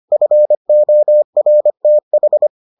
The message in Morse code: FORTH

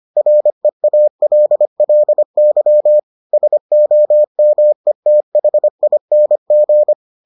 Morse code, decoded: REALLY SOMETHING

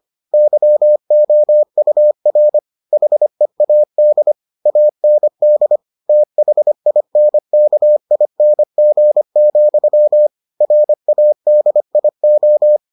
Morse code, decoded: YOUR HEAD AND THINKING, RADIO

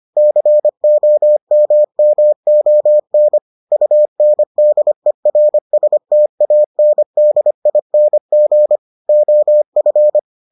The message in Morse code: COMMON UNDERSTANDING OF

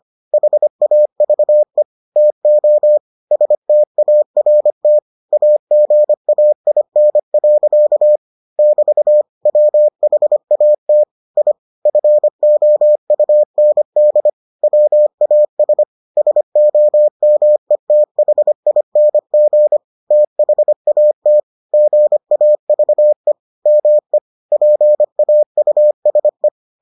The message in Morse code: HAVE TO START AGAIN. <BT> WHAT I FOUND WAS SOMETHING THAT GAVE ME PAUSE

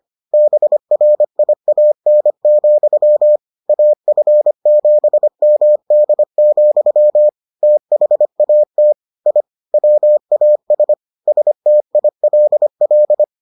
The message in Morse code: BRIAN, AF7MD, THAT I WAS STILL

